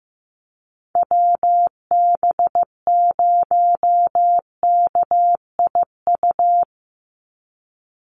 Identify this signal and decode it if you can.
Morse code, heard as a keyed tone, about 15 wpm, 700 Hz. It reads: WB0KIU
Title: NA SOTA Chaser Callsigns 15wpm